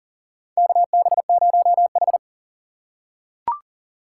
Morse code, heard as a keyed tone, about 40 wpm, 700 Hz. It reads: KB0H
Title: USA Call Signs - 4 Characters - 3x Character Spacing - 40wpm